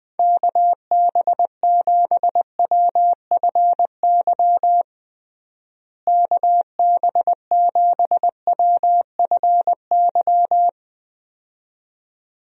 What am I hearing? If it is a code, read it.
Morse code, heard as a keyed tone, about 21 wpm, 700 Hz. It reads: KB7WFY KB7WFY
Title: USA Call Signs - 6 Characters - Repeated 2x - 20wpm